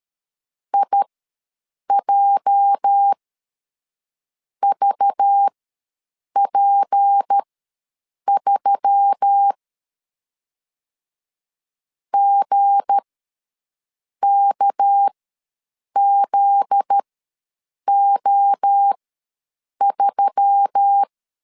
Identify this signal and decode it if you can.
Morse code, heard as a keyed tone, about 13 wpm, spaced as for 6 wpm, 800 Hz. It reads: IJVP3 GKZO3
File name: LEZIONI / 9 28 ws 12wpm 3354